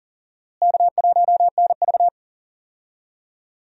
Morse code, heard as a keyed tone, about 40 wpm, 700 Hz. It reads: K1NV
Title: USA Call Signs - 4 Characters - Repeated 3x - 40wpm